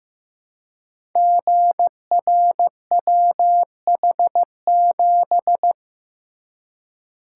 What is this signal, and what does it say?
Morse code, heard as a keyed tone, about 15 wpm, 700 Hz. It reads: GRWH7
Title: Code Groups - 5 Letter:Numbers - 15wpm